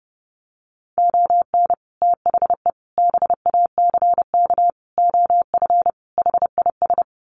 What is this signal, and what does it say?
Morse code, heard as a keyed tone, about 30 wpm, 700 Hz. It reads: ON THE BACK OF HIS
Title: Sets of 5 Words Encore 30wpm